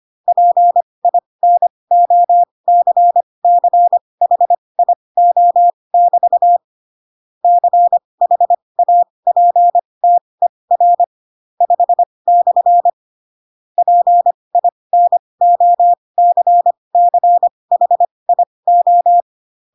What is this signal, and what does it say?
Morse code, heard as a keyed tone, about 25 wpm, spaced as for 20 wpm, 700 Hz. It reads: PINOCCHIO<BT> CHAPTER 5/ PINOCCHIO
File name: Pinocchio / Pinocchio ch5 15 25 ik1hge